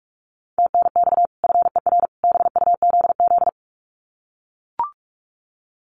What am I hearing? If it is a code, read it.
Morse code, heard as a keyed tone, about 44 wpm, 700 Hz. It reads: TNX FER BUZZ
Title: Common QSO Phrases 45wpm